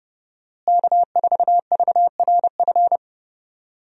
Morse code, heard as a keyed tone, about 30 wpm, 700 Hz. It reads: K4VRF
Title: USA Call Signs - 5 Characters - Repeated 2x - 30wpm